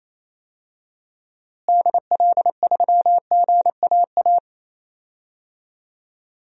Morse code, heard as a keyed tone, about 28 wpm, 700 Hz. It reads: DL3GAA
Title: DXCC-100 Call Signs - 6 Characters - Repeated 3x - 28wpm